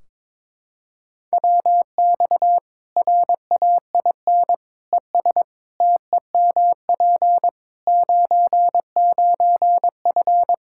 Morse code, heard as a keyed tone, about 22 wpm, 700 Hz. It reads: WX RAIN ES TEMP 99F
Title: Practice QSO - Exchange 2 - Info - 22wpm